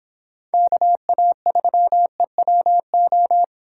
Morse code, heard as a keyed tone, about 26 wpm, 700 Hz. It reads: KA3EWO